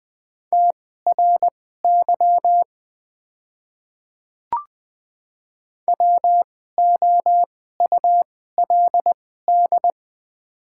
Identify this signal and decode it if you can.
Morse code, heard as a keyed tone, about 20 wpm, spaced as for 14 wpm, 700 Hz. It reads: TRY WOULD
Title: 2x Character Spacing - Top 300 Words - 20wpm